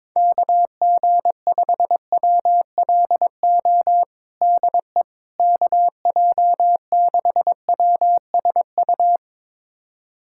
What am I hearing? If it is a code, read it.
Morse code, heard as a keyed tone, about 22 wpm, 700 Hz. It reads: KG5WLO DE KJ6WSU
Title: USA Callsign DE Callsign 22wpm